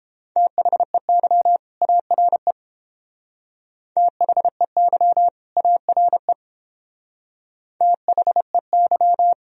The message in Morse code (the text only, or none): THEY ARE THEY ARE THEY